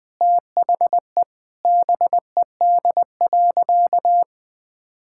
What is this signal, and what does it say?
Morse code, heard as a keyed tone, about 20 wpm, 700 Hz. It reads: THE BED.
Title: Sentences from Top 1000 Words - Review Part 6 - 20wpm